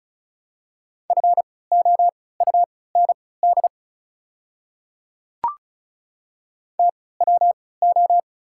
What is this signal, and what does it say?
Morse code, heard as a keyed tone, about 34 wpm, spaced as for 20 wpm, 700 Hz. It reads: FOUND TWO
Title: 3x Character Spacing - Top 300 Words - 35wpm